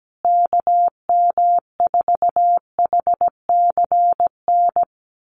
Morse code, heard as a keyed tone, about 17 wpm, 700 Hz. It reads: KM4HCN
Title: USA Call Signs Repeated 2x 17wpm